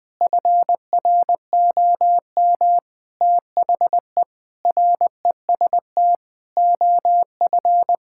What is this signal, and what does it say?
Morse code, heard as a keyed tone, about 20 wpm, 700 Hz. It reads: FROM THE REST OF
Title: Sentences with 5-Word Phrases - 20wpm